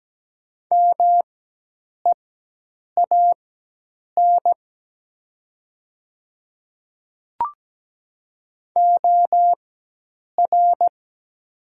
Morse code, heard as a keyed tone, about 17 wpm, spaced as for 8 wpm, 700 Hz. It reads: MEAN OR
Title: 4x Character Spacing - Top 200 Words - 17wpm